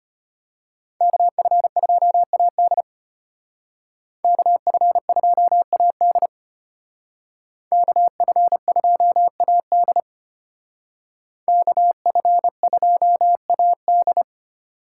Morse code, heard as a keyed tone, about 33 wpm, 700 Hz. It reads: KF2AD KF2AD KF2AD KF2AD